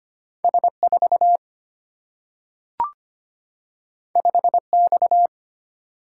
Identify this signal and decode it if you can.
Morse code, heard as a keyed tone, about 25 wpm, 700 Hz. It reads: S4 5X
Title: Single-Letter Single-Number 25wpm